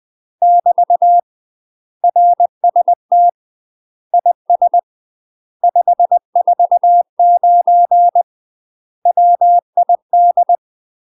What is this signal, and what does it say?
Morse code, heard as a keyed tone, about 20 wpm, 700 Hz. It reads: <BT> RST IS 549 WID
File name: UK MASTER QSOs TRAINING / 20WPM / UK MASTER QSOs 0007